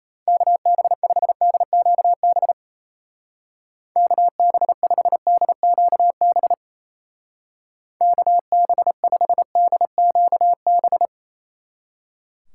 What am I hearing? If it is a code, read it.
Morse code, heard as a keyed tone, about 34 wpm, 700 Hz. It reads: KB5DQB KB5DQB KB5DQB
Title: Speed Racing - USA Call Signs 28wpm